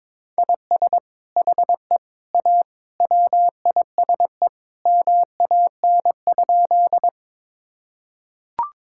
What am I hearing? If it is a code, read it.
Morse code, heard as a keyed tone, about 22 wpm, 700 Hz. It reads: IS HE A WISE MAN?
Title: Sentences from Top 1600 Words 22wpm